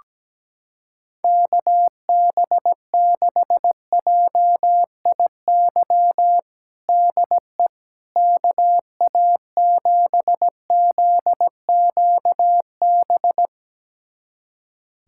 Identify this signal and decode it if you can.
Morse code, heard as a keyed tone, about 17 wpm, 700 Hz. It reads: KB6JIY DE KA7ZQB
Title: USA Callsign DE Callsign - 6 Characters - 17wpm